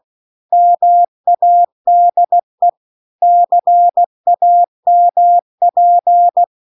Morse code, heard as a keyed tone, about 16 wpm, 700 Hz. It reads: MADE CAMP